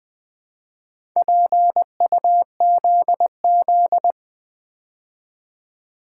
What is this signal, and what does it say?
Morse code, heard as a keyed tone, about 20 wpm, 700 Hz. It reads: PUZZ
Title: Build A Word - 2 Letters At A Time - 6-Letter Words - 20wpm